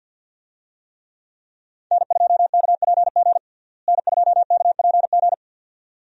Morse code, heard as a keyed tone, about 52 wpm, 700 Hz. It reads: N2KPG N2KPG
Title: USA Call Signs - 5 Characters - Repeated 2x - 50wpm